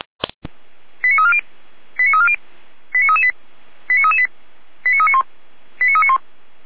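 Five-tone selective call (ZVEI), heard as quick runs of five tones, tones of about 70 ms. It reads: 89340 89340 89338 89338 89391 89391
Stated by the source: ZVEI five-tone selective call 89327, received on an SDR.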